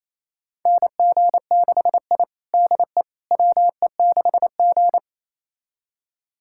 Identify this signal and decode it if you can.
Morse code, heard as a keyed tone, about 28 wpm, 700 Hz. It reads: NG6I DE WE6G
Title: USA Callsign DE Callsign - 4 Characters - 28wpm